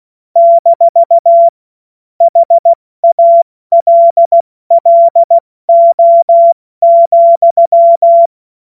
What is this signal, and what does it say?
Morse code, heard as a keyed tone, about 16 wpm, 680 Hz. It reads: - HALLO,
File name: Morsekurs / Witze / Witz 03. Mai 2022 70 F80